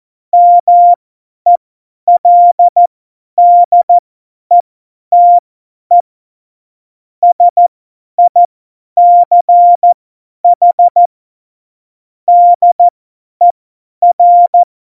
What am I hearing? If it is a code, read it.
Morse code, heard as a keyed tone, about 14 wpm, spaced as for 10 wpm, 700 Hz. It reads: MELDETE SICH DER
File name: Morsekurs / Witze / Witz 18. Februar 2020, 50 F70